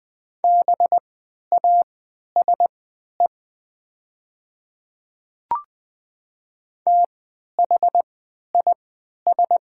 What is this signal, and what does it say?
Morse code, heard as a keyed tone, about 20 wpm, spaced as for 11 wpm, 690 Hz. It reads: BASE THIS
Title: 3x Character Spacing - Top 300 Words - 20wpm